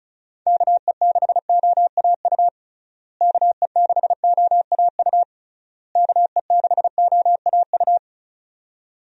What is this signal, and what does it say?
Morse code, heard as a keyed tone, about 34 wpm, 700 Hz. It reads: KE6OAU KE6OAU KE6OAU
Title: USA Call Signs - 6 Characters - Repeated 3x - 35wpm